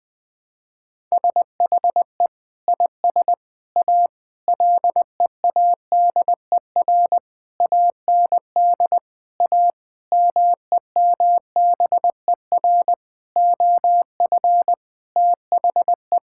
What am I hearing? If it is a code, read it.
Morse code, heard as a keyed tone, about 20 wpm, 700 Hz. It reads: SHE IS A LEADER AND A MEMBER OF THE
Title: Sentences with 5-Word Phrases - 20wpm